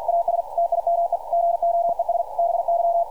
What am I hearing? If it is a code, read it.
Morse code, heard as a keyed tone, about 52 wpm, 680 Hz, with radio noise. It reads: AS TIME GOES BY
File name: archived / CRSnet